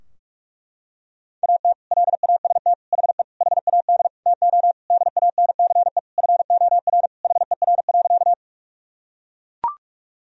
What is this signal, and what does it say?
Morse code, heard as a keyed tone, about 41 wpm, 700 Hz. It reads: AT LAST HE HAD TO DANCE FOR HER.
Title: Sentences from Top 1000 Words - Review Part 4 - 45wpm